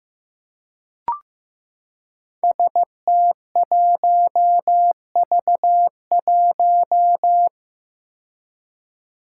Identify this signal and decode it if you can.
Morse code, heard as a keyed tone, about 15 wpm, 700 Hz. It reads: ST1V1
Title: Five Challenging Letter-Number Combinations 15wpm